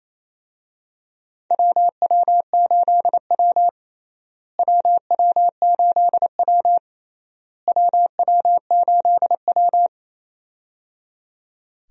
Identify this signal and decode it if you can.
Morse code, heard as a keyed tone, about 28 wpm, 700 Hz. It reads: WW8W WW8W WW8W
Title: USA Call Signs - 4 Characters - Repeated 3x - 28wpm